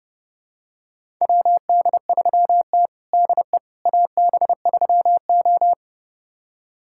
Morse code, heard as a keyed tone, about 30 wpm, 700 Hz. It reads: WD3T DE AB3O
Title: USA Callsign DE Callsign - 4 Characters - 30wpm